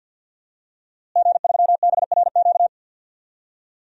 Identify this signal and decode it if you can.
Morse code, heard as a keyed tone, about 51 wpm, 700 Hz. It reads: G3BRQ
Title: DXCC-100 Call Signs - 5 Characters - Repeated 2x - 50wpm